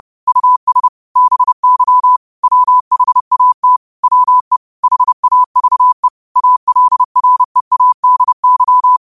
Morse code, heard as a keyed tone, about 30 wpm, 1000 Hz. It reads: US BY WHAT WE HAVE ALREADY